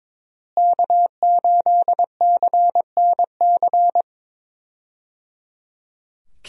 Morse code, heard as a keyed tone, about 22 wpm, 700 Hz. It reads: K8CNC
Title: USA Call Signs - 5 Characters - Repeated 3x - 22wpm